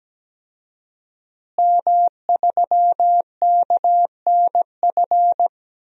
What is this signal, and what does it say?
Morse code, heard as a keyed tone, about 17 wpm, 700 Hz. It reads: M3KNF
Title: DXCC-100 Call Signs - 5 Characters - Repeated 2x - 17wpm